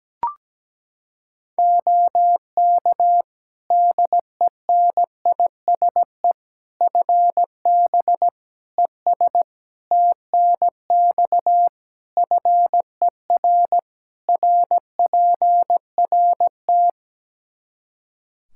Morse code, heard as a keyed tone, about 17 wpm, 700 Hz. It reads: OK DENISE FB ES TNX FER RPRT
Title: K1USN SST QSO Practice 17wpm